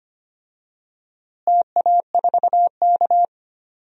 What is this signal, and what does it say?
Morse code, heard as a keyed tone, about 25 wpm, 700 Hz. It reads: TA4K
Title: DXCC-100 Call Signs - 4 Characters - Repeated 2x - 25wpm